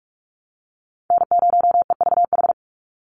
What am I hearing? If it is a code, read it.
Morse code, heard as a keyed tone, about 46 wpm, 690 Hz. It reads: N0EVH
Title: NA SOTA Chaser Callsigns 45wpm